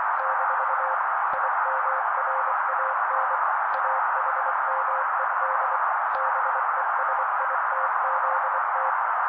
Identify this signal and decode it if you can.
Morse code, heard as a keyed tone, about 24 wpm, 530 Hz. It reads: - IMRAN AHMED BESITZT
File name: daily / 23wpm noise